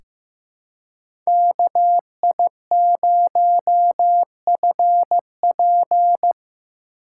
Morse code, heard as a keyed tone, about 15 wpm, 700 Hz. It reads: KI0FP
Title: USA Call Signs - 3x Character Spacing - 15wpm